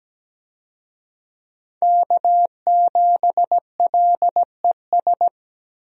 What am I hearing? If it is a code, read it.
Morse code, heard as a keyed tone, about 17 wpm, 700 Hz. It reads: K7LES
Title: USA Call Signs - 5 Characters - 17wpm